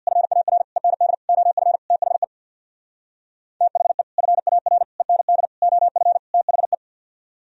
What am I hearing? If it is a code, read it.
Morse code, heard as a keyed tone, about 65 wpm, 700 Hz. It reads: FAR END OF THE THE FAR END OF THE